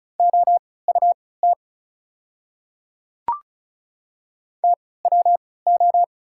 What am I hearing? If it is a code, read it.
Morse code, heard as a keyed tone, about 34 wpm, spaced as for 20 wpm, 690 Hz. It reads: OUT TWO
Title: 3x Character Spacing - Top 100 Words - 35wpm